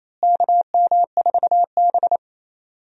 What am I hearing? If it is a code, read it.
Morse code, heard as a keyed tone, about 28 wpm, 700 Hz. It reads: KM4B
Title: USA Call Signs - 4 Characters - Repeated 3x - 28wpm